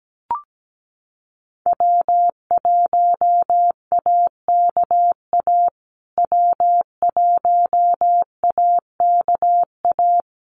Morse code, heard as a keyed tone, about 17 wpm, 700 Hz. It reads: W1AKA W1AKA
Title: USA Call Signs Repeated 3x 17wpm